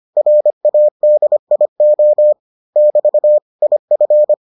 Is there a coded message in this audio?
Yes: Morse code from 0.2 to 4.3 s, heard as a keyed tone, about 25 wpm, 600 Hz. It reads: RADIO <BT> IF